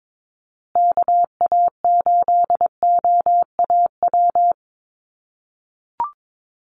Morse code, heard as a keyed tone, about 22 wpm, 700 Hz. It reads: KA8OAW
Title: USA Call Signs Repeated 3x 22wpm